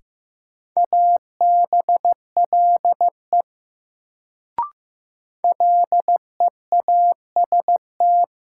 Morse code, heard as a keyed tone, about 15 wpm, 700 Hz. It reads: ABLE LEAST
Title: Sentences from Top 700 Words - New Word Drill - 15wpm